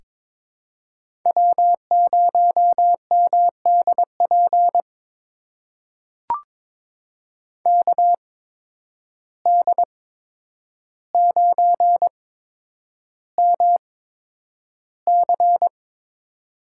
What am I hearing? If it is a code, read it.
Morse code, heard as a keyed tone, about 22 wpm, spaced as for 9 wpm, 700 Hz. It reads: W0MDP KD9MC
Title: USA Call Signs - 5 Characters - 8x Character Spacing - 22wpm